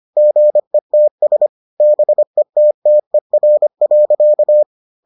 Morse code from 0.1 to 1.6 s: GETS

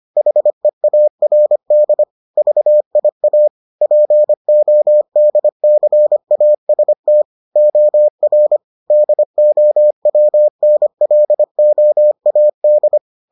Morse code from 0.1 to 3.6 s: HEARD VIA

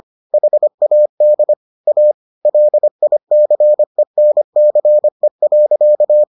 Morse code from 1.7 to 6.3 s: A LICENCE.

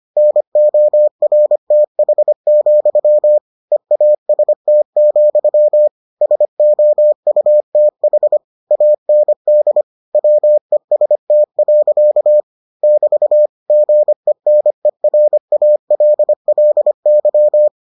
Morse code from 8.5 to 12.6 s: AND WEST.

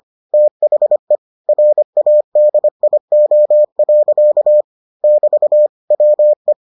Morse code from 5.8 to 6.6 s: WE

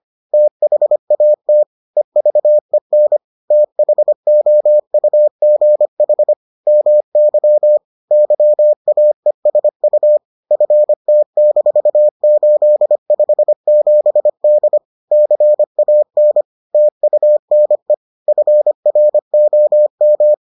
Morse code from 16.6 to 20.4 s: TUNE FROM

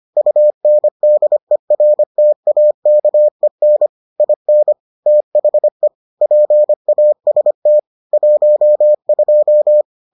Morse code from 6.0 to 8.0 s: PAST

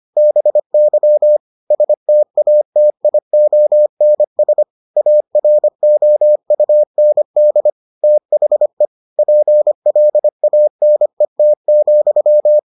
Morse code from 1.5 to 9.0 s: STATIONS AROUND THE

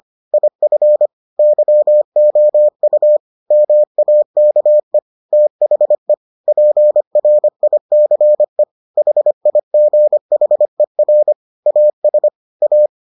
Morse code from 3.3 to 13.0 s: MAKE THE PRICE HIGHER AS A

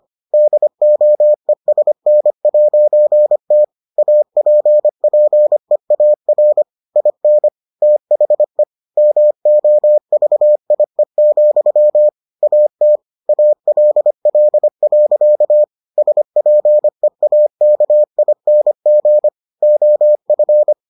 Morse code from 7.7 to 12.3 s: THE MOVIE,

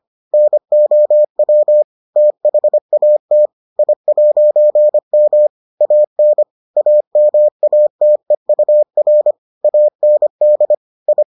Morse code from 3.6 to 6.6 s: I'M AN

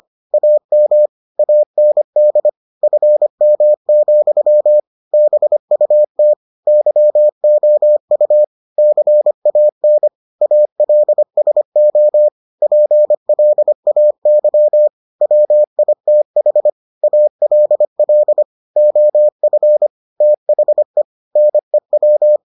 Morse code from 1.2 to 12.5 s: AND FM, BUT YOU CAN ALSO

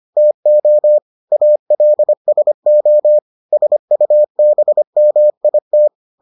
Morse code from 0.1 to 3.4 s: TO ALSO

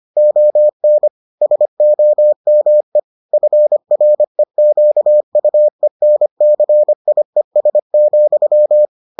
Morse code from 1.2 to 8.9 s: SOME FREQUENCIES,

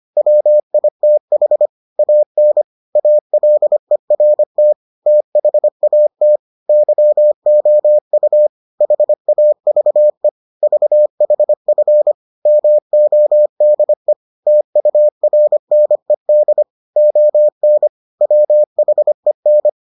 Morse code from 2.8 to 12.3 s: ALERT THAT YOU HAVE VHF

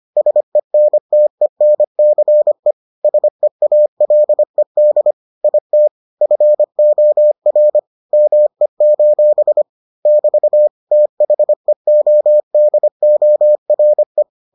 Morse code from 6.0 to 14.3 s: FOR ME: <BT> THEODORE

